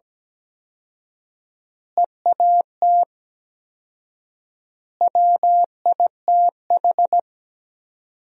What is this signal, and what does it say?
Morse code, heard as a keyed tone, about 17 wpm, 700 Hz. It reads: EAT WITH
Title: 4x Word-Spacing - Sentences from Top 300 Words - 17wpm